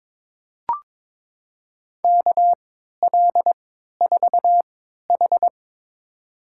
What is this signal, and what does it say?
Morse code, heard as a keyed tone, about 21 wpm, spaced as for 12 wpm, 700 Hz. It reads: KL4H
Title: USA Call Signs - 4 Characters - 3x Character Spacing - 22wpm